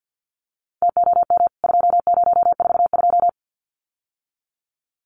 Morse code, heard as a keyed tone, about 50 wpm, 700 Hz. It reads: TOM 2042
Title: CWT Exchange 50wpm